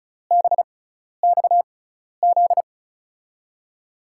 Morse code, heard as a keyed tone, about 36 wpm, spaced as for 12 wpm, 700 Hz. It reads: BXZ
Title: USA Call Signs - 6x Character Spacing - 35wpm